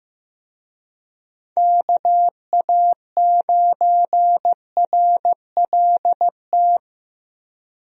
Morse code, heard as a keyed tone, about 15 wpm, 700 Hz. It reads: KA9RLT
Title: USA Call Signs - 6 Characters - 15wpm